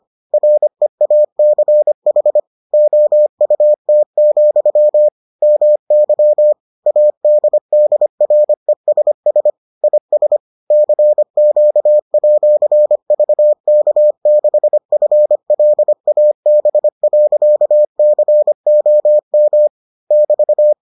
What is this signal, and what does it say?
Morse code, heard as a keyed tone, about 25 wpm, 600 Hz. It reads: REACH OUT, MY ADDRESS IS CQ@VK6FLAB.COM <BT>